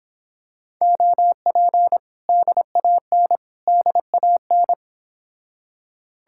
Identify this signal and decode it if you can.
Morse code, heard as a keyed tone, about 26 wpm, 700 Hz. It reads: OP DAN DAN